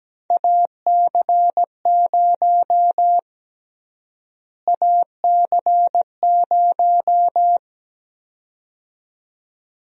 Morse code, heard as a keyed tone, about 17 wpm, 700 Hz. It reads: AC0 AC0
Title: USA Call Sign Prefixes - Repeated 2x - 17wpm